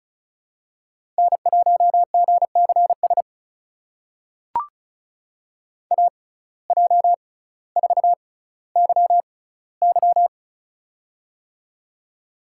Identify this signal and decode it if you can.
Morse code, heard as a keyed tone, about 36 wpm, spaced as for 18 wpm, 700 Hz. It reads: N1GCS AJ4YY
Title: USA Call Signs - 6x Character Spacing - 35wpm